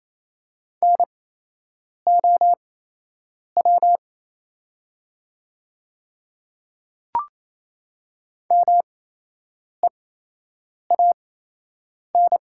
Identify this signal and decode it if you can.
Morse code, heard as a keyed tone, about 28 wpm, spaced as for 8 wpm, 690 Hz. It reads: NOW MEAN
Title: 8x Character Spacing - Top 200 Words - 28wpm